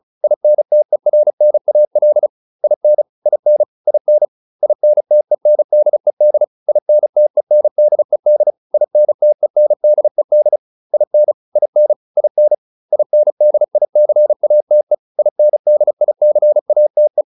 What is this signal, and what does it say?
Morse code, heard as a keyed tone, about 34 wpm, 600 Hz. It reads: INTERNAL IN IN IN INTENDED INTENDED INTENDED IN IN IN INDICATE INDICATE